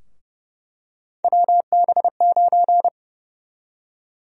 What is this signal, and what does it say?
Morse code, heard as a keyed tone, about 32 wpm, 700 Hz. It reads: WB9
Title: USA Call Sign Prefixes - Repeated 2x - 30wpm